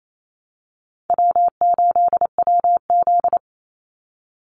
Morse code, heard as a keyed tone, about 28 wpm, 700 Hz. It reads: W8WZ
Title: Easy Call Signs Repeated 3x - 28wpm